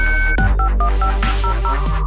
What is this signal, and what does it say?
DTMF, heard as telephone touch tones, tones of about 100 ms. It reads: 651500